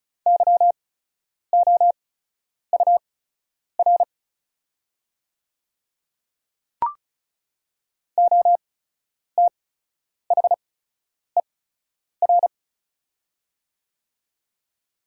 Morse code, heard as a keyed tone, about 34 wpm, spaced as for 10 wpm, 690 Hz. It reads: YOUR OTHER
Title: 8x Character Spacing - Top 100 Words - 35wpm